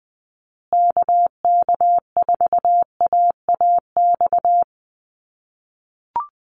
Morse code, heard as a keyed tone, about 20 wpm, 700 Hz. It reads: KK4AAX
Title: USA Call Signs Repeated 3x 20wpm